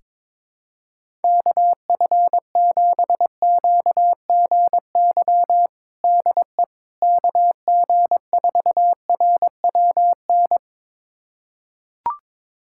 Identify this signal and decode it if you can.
Morse code, heard as a keyed tone, about 22 wpm, 700 Hz. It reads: KF7QGY DE KG4RWN
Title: USA Callsign DE Callsign - 6 Characters - 22wpm